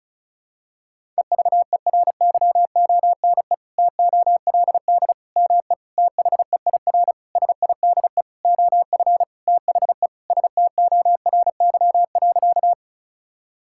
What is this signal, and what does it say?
Morse code, heard as a keyed tone, about 36 wpm, 700 Hz. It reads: EVERYONE TOLD ME THEIR SIDE OF THE STORY.
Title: Sentences from Top 1000 Words - Review Part 5 - 35wpm